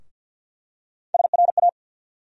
Morse code, heard as a keyed tone, about 51 wpm, 700 Hz. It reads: SRA
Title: USA Call Sign Postfixes - 3 Letters 50wpm